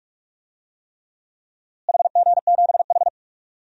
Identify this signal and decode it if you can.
Morse code, heard as a keyed tone, about 48 wpm, 700 Hz. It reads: HG7H
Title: DXCC-100 Call Signs - 4 Characters - Repeated 3x - 45wpm